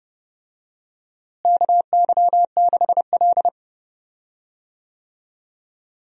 Morse code, heard as a keyed tone, about 30 wpm, 700 Hz. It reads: KY6L
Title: USA Call Signs - 4 Characters - Repeated 2x - 30wpm